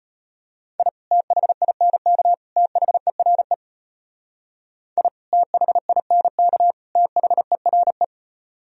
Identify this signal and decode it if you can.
Morse code, heard as a keyed tone, about 31 wpm, 700 Hz. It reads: I THINK THERE I THINK THERE
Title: Speed-Racing - Sets of 3 Words 25wpm